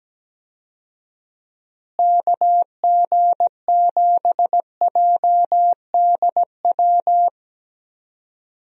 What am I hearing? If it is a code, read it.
Morse code, heard as a keyed tone, about 17 wpm, 700 Hz. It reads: KG7JDW
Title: USA Call Signs - 6 Characters - Repeated 2x - 17wpm